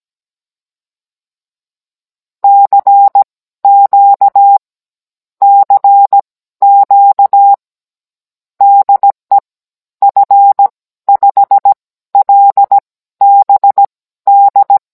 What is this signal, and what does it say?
Morse code, heard as a keyed tone, about 17 wpm, spaced as for 13 wpm, 800 Hz. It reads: CQ CQ DE F5LBD
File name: Cours-NR19-16mm